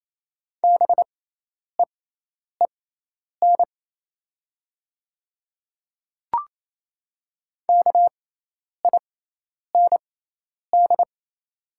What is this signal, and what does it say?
Morse code, heard as a keyed tone, about 28 wpm, spaced as for 10 wpm, 690 Hz. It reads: BEEN KIND
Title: 6x Character Spacing - Top 200 Words - 28wpm